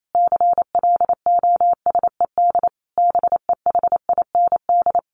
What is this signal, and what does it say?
Morse code, heard as a keyed tone, about 28 wpm, 700 Hz. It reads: CLOSED BEHIND
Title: Sentences from Top 1400 Words 28wpm